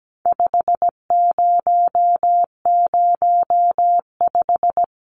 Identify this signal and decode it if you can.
Morse code, heard as a keyed tone, about 17 wpm, 700 Hz. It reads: 5005
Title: Numbers - 4-Digit 17wpm